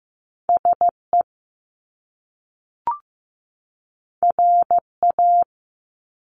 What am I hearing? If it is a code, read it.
Morse code, heard as a keyed tone, about 15 wpm, 700 Hz. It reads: SE RA
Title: Top Two Letter Combinations in Words 15wpm